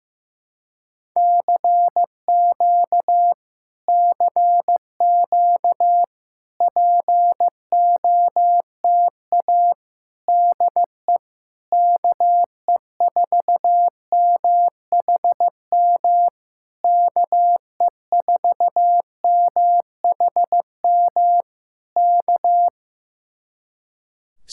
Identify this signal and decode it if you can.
Morse code, heard as a keyed tone, about 15 wpm, 700 Hz. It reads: CQ CQ POTA DE KE4MHM KE4MHM K
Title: Speed-Racing Common QSO Phrases Encore - 15wpm